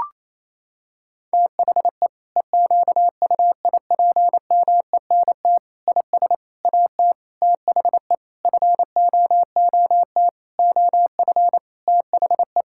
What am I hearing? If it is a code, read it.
Morse code, heard as a keyed tone, about 28 wpm, 700 Hz. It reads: THE EQUIPMENT IS AT THE FOOT OF THE
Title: Sentences from Top 1000 Words - Review Part 5 - 28wpm